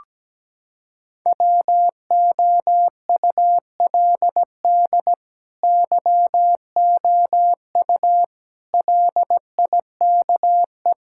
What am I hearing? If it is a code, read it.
Morse code, heard as a keyed tone, about 17 wpm, 700 Hz. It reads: WOULD YOU LIKE
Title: Sentences from Top 1000 Words - Review Part 6 - 17wpm